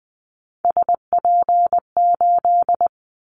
Morse code, heard as a keyed tone, about 20 wpm, 700 Hz. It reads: SP8
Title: Callsign Prefixes for Top 25 DXCC - Repeated 3x - 20wpm